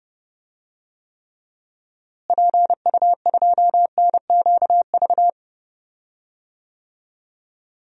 Morse code, heard as a keyed tone, about 30 wpm, 700 Hz. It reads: PU2NQV
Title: DXCC-100 Call Signs - 6 Characters - Repeated 3x - 30wpm